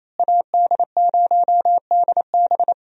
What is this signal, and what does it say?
Morse code, heard as a keyed tone, about 28 wpm, 700 Hz. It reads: AD0DB
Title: USA Call Signs - 5 Characters - Repeated 2x - 28wpm